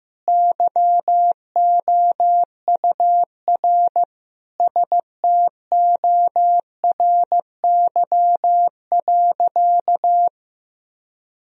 Morse code, heard as a keyed tone, about 15 wpm, 700 Hz. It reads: YOUR STORY.
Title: Sentences from Top 1000 Words - Review Part 3 - 15wpm